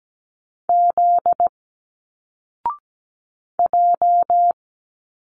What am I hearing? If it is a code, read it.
Morse code, heard as a keyed tone, about 17 wpm, 700 Hz. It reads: Z J
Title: Single Letters 17wpm